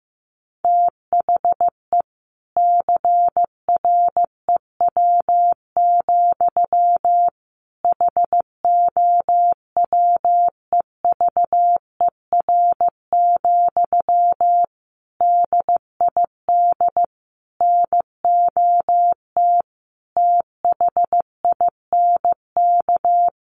Morse code, heard as a keyed tone, about 15 wpm, 700 Hz. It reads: THE CREW, HOWEVER, DID NOT THINK
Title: Sentences from Top 1500 Words 15wpm